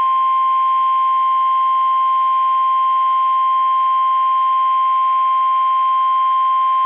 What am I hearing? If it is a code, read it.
Emergency Alert System (SAME) tones (the 1050 Hz attention tone). It is heard from 0.4 to 5.9 s.
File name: eas / KUSO / KUSO